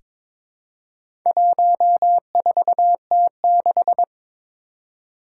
Morse code, heard as a keyed tone, about 22 wpm, 700 Hz. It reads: 14T6
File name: practice / 22wpm / Four Letter-Number Combinations - 22wpm